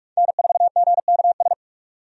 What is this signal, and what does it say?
Morse code, heard as a keyed tone, about 46 wpm, 700 Hz. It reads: N4GKS
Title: USA Call Signs - 5 Characters - 45wpm